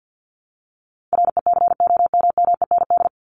Morse code, heard as a keyed tone, about 50 wpm, 700 Hz. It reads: RECOMMEND
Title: Top 1001-1250 Words 50wpm